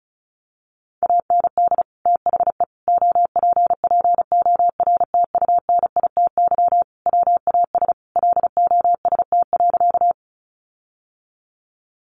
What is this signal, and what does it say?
Morse code, heard as a keyed tone, about 34 wpm, 700 Hz. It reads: AND THE OPPORTUNITY WAS LOST.
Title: Sentences from Top 1400 Words 35wpm